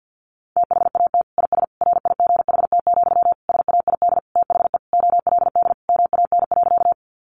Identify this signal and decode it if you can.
Morse code, heard as a keyed tone, about 50 wpm, 700 Hz. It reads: THAT IS RIGHT, SAID THE OLD MAN.
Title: Sentences from Top 200 Words 50wpm